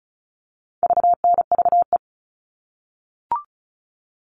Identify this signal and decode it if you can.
Morse code, heard as a keyed tone, about 34 wpm, 700 Hz. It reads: VNVE
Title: All Cut Numbers - 4-Digit 35wpm